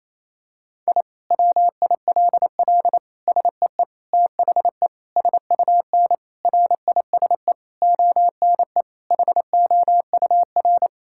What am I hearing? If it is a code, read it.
Morse code, heard as a keyed tone, about 28 wpm, 700 Hz. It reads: I WILL SEE THE SUN RISE ONE HOUR
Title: Sentences from Top 1000 Words - Review Part 6 - 28wpm